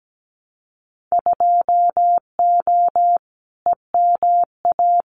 Morse code, heard as a keyed tone, about 17 wpm, 700 Hz. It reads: 2O EMA
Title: Winter Field Day Exchanges 17wpm